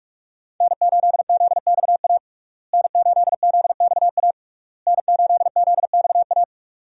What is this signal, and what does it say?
Morse code, heard as a keyed tone, about 46 wpm, 700 Hz. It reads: N8ZXA N8ZXA N8ZXA
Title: USA Call Signs - 5 Characters - Repeated 3x - 45wpm